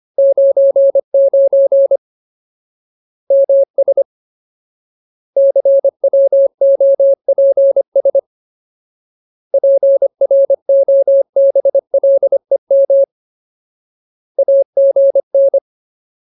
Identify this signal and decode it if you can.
Morse code, heard as a keyed tone, about 25 wpm, 550 Hz. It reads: 99 MS CWOPS PROBLEM AGN
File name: Ultimate Word List / 25WPM Ultimate